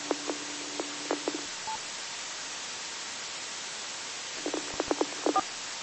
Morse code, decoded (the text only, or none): none